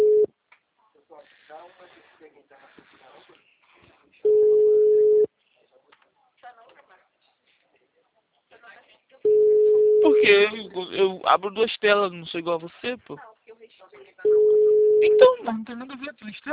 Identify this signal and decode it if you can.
Ringback tone (Europe, 425 Hz), with speech.